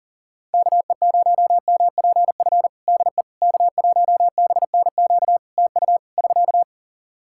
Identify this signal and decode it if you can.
Morse code, heard as a keyed tone, about 40 wpm, 700 Hz. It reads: KE0MPF DE K1BNQ TU <SK>
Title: Practice QSO - Exchange 4 - Ending - 40wpm